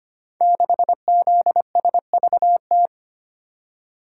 Morse code, heard as a keyed tone, about 25 wpm, 700 Hz. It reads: 6ZSVT
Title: Code Groups - 5 Letter:Numbers - 25wpm